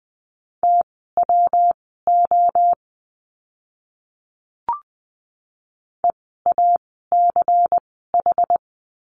Morse code, heard as a keyed tone, about 20 wpm, spaced as for 14 wpm, 700 Hz. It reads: TWO EACH
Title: 244 Lesson 39f - Letter z - Words 20wpm